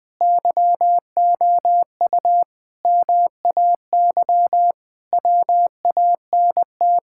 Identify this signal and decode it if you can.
Morse code, heard as a keyed tone, about 20 wpm, 700 Hz. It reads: YOU MAY WANT
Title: Sets of 8 Words - 20wpm